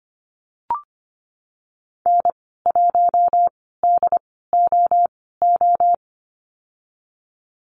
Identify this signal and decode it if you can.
Morse code, heard as a keyed tone, about 25 wpm, spaced as for 16 wpm, 700 Hz. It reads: N1DOO
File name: Lesson 15g - Number 5 - Call Signs 25wpm